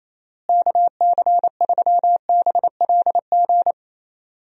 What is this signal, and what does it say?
Morse code, heard as a keyed tone, about 27 wpm, 700 Hz. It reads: KC3BLG
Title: USA Call Signs - 6 Characters - Repeated 3x - 28wpm